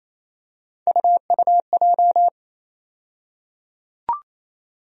Morse code, heard as a keyed tone, about 28 wpm, 700 Hz. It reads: UUJ
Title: USA Call Sign Postfixes - 3 Letters 28wpm